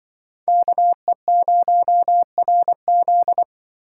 Morse code, heard as a keyed tone, about 25 wpm, 700 Hz. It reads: KE0RZ
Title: Speed Racing - USA Call Signs 20wpm